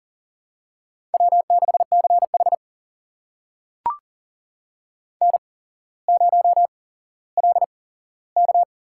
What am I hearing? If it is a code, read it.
Morse code, heard as a keyed tone, about 38 wpm, spaced as for 17 wpm, 700 Hz. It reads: W6CH N0LK
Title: USA Call Signs - 4 Characters - 8x Character Spacing - 40wpm